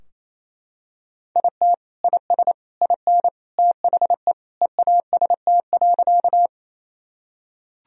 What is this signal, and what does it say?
Morse code, heard as a keyed tone, about 28 wpm, 700 Hz. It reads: IT IS IN THE EAST.
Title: Sentences from Top 1000 Words - Review Part 3 - 28wpm